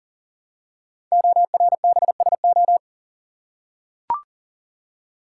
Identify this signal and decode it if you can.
Morse code, heard as a keyed tone, about 39 wpm, 700 Hz. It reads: ORBSO
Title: Code Groups - 5 Letter:Numbers - 40wpm